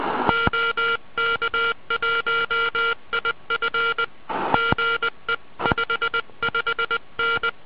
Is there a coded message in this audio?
Yes: Morse code from 0.3 to 7.5 s, heard as a keyed tone, about 20 wpm, 1390 Hz, with noise. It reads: OK1IF GE 55N